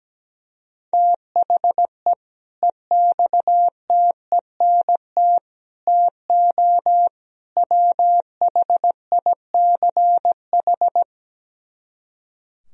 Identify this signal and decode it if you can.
Morse code, heard as a keyed tone, about 17 wpm, 700 Hz. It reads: THE EXTENT TO WHICH
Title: Speed-Racing - Sets of 4 Words 17wpm